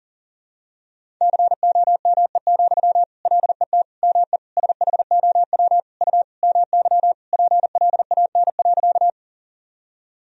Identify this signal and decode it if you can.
Morse code, heard as a keyed tone, about 40 wpm, 700 Hz. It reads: COME, LET ME SHOW U MY PLAN.
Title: Sentences from Top 500 Words and Abbreviations 40wpm